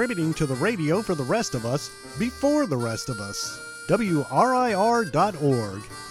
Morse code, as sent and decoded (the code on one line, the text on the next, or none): none